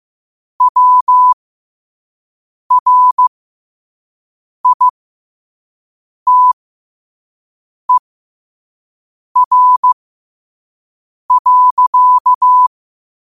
Morse code, heard as a keyed tone, about 15 wpm, spaced as for 5 wpm, 1000 Hz. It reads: WRITER.
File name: qotdfiles / 5wpm